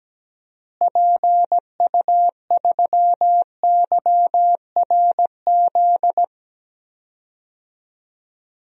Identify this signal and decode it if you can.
Morse code, heard as a keyed tone, about 17 wpm, 700 Hz. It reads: PU3YRZ
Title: DXCC-100 Call Signs - 6 Characters - Repeated 3x - 17wpm